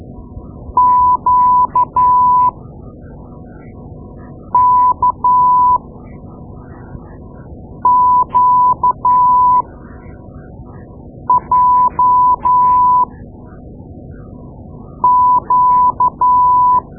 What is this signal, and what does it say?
Morse code, heard as a keyed tone, about 12 wpm, spaced as for 4 wpm, 1000 Hz. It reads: QKQJQ
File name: Morse Lessons / Morse Lessons / Tape-2A